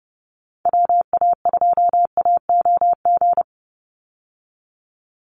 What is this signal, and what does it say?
Morse code, heard as a keyed tone, about 30 wpm, 700 Hz. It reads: WA2AOG
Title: Call Signs - 30wpm